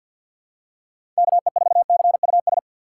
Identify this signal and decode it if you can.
Morse code, heard as a keyed tone, about 47 wpm, 700 Hz. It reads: KE4CUS
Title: USA Call Signs - 4x Character Spacing - 50wpm